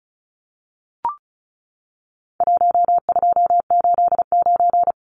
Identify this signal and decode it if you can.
Morse code, heard as a keyed tone, about 34 wpm, 700 Hz. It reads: 1289
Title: Numbers - Contest Style - 4-Digit 35wpm